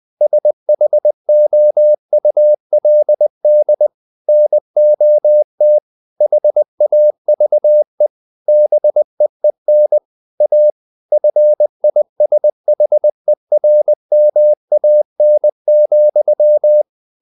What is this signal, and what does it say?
Morse code, heard as a keyed tone, about 20 wpm, 600 Hz. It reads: SHOULD NOT HAVE BEEN A FISHERMAN,
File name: the old man and the sea / the old man and the sea part 1 20wpm